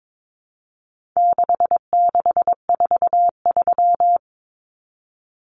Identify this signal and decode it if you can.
Morse code, heard as a keyed tone, about 22 wpm, 700 Hz. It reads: 6643
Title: Common Cut Numbers - 4-Digit 22wpm